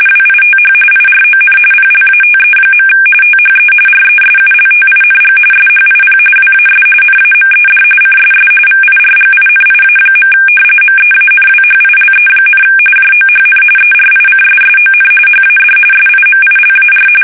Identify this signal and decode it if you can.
RTTY, about 45.45 baud. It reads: WITH IRON PLATE. THE RESULTING SHIP WAS NAMED CSS VIRGINIA. MEANWHILE, JOHN ERICSSON HAD SIMILAR
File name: fsk850 / usnavy3